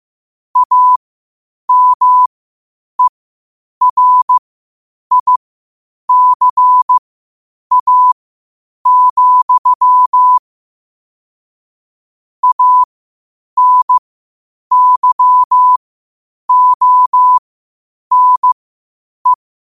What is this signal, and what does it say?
Morse code, heard as a keyed tone, about 15 wpm, spaced as for 8 wpm, 1000 Hz. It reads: AMERICA, ANYONE
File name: qotdfiles / 8wpm